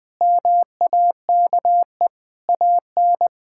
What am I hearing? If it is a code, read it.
Morse code, heard as a keyed tone, about 20 wpm, 700 Hz. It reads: MAKE AN
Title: Sentences from Top 1000 Words - Review Part 6 - 20wpm